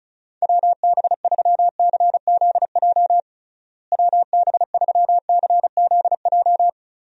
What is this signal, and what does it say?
Morse code, heard as a keyed tone, about 34 wpm, 700 Hz. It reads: WB3CZJ WB3CZJ
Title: USA Call Signs - 6 Characters - Repeated 2x - 35wpm